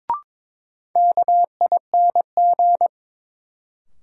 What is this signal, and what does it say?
Morse code, heard as a keyed tone, about 23 wpm, 700 Hz. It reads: KING
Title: Sentences from Top 400 Words - New Word Drill - 22wpm